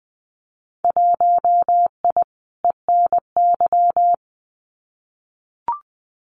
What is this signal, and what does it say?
Morse code, heard as a keyed tone, about 20 wpm, 700 Hz. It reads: 1I ENY
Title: Winter Field Day Exchanges Encore 20wpm